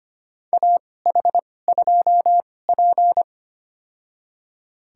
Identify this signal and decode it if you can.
Morse code, heard as a keyed tone, about 25 wpm, spaced as for 18 wpm, 700 Hz. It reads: AH2P
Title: USA Call Signs - 2x Character Spacing - 25wpm